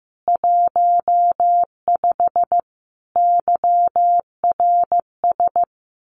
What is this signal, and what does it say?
Morse code, heard as a keyed tone, about 15 wpm, 700 Hz. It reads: 15 YRS
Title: Common QSO Phrases Encore 15wpm